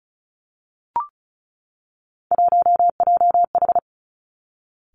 Morse code, heard as a keyed tone, about 36 wpm, 700 Hz. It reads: 1JH
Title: Three Challenging Letter-Number Combinations 35wpm